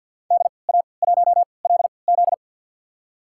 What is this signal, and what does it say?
Morse code, heard as a keyed tone, about 46 wpm, spaced as for 27 wpm, 700 Hz. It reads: DA1LZ
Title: USA Call Signs - 3x Character Spacing - 50wpm